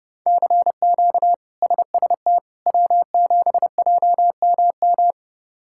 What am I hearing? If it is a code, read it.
Morse code, heard as a keyed tone, about 32 wpm, 700 Hz. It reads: CQ SST W7JMM
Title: K1USN SST QSO Practice - 30wpm